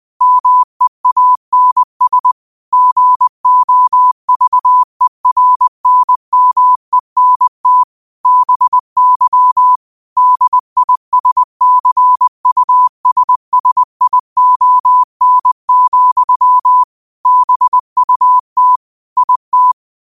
Morse code, heard as a keyed tone, about 20 wpm, 1000 Hz. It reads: MEANS GOVERNMENT BY DISCUSSION, BUT IT